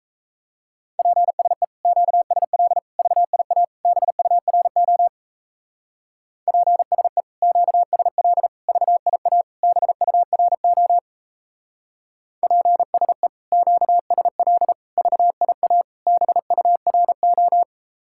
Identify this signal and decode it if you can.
Morse code, heard as a keyed tone, about 36 wpm, 700 Hz. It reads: PSE QSL VIA BURO PSE QSL VIA BURO PSE QSL VIA BURO